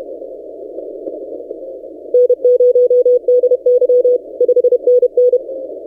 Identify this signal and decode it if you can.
Morse code, heard as a keyed tone, about 30 wpm, 500 Hz. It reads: N0DY 5NN
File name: posts / ft5zm-40cw- -1425z